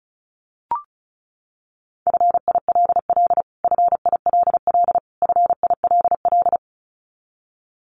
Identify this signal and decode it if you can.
Morse code, heard as a keyed tone, about 34 wpm, 700 Hz. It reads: FILL FILL FILL
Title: Top 500 Words Repeated 35wpm